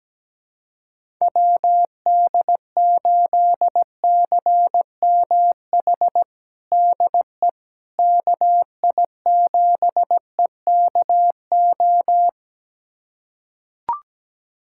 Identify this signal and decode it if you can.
Morse code, heard as a keyed tone, about 17 wpm, 700 Hz. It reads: WD8CMH DE KI7EKO
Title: USA Callsign DE Callsign 17wpm